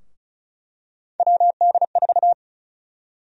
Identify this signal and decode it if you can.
Morse code, heard as a keyed tone, about 36 wpm, 700 Hz. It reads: WD4
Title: USA Call Sign Prefixes - 35wpm